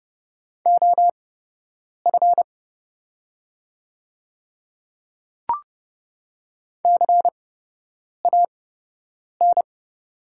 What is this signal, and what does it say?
Morse code, heard as a keyed tone, about 30 wpm, spaced as for 8 wpm, 700 Hz. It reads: OF CAN
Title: 8x Character Spacing - Top 100 Words - 30wpm